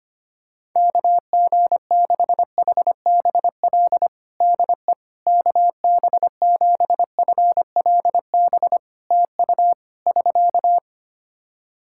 Morse code, heard as a keyed tone, about 25 wpm, 700 Hz. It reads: KG6HBL DE KB7FLB TU <SK>
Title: Practice QSO - Exchange 4 - Ending - 25wpm